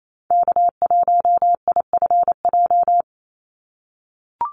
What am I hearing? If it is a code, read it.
Morse code, heard as a keyed tone, about 28 wpm, 700 Hz. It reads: K1IFJ
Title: Call Signs - 28wpm